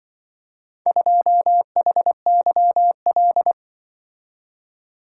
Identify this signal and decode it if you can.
Morse code, heard as a keyed tone, about 25 wpm, 700 Hz. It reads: 2HYL
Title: Speed-Racing - Four Letter-Number Combinations - 20wpm